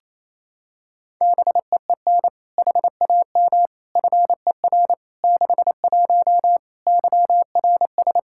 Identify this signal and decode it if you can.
Morse code, heard as a keyed tone, about 28 wpm, 700 Hz. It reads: BEEN HAM FER 61 YRS
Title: Practice QSO - Exchange 3 - More Info - 28wpm